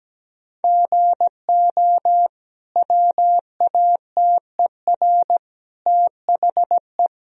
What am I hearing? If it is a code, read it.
Morse code, heard as a keyed tone, about 17 wpm, 700 Hz. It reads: GO WATER THE
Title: Sentences from Top 1000 Words - Review Part 1 - 17wpm